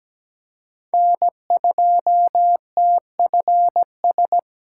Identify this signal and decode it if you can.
Morse code, heard as a keyed tone, about 17 wpm, 700 Hz. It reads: N2TFS
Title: USA Call Signs - 8x Character Spacing - 17wpm